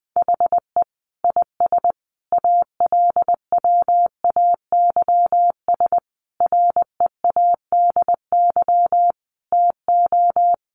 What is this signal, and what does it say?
Morse code, heard as a keyed tone, about 20 wpm, 700 Hz. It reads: HE IS ALWAYS READY TO
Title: Sentences from Top 1300 Words 20wpm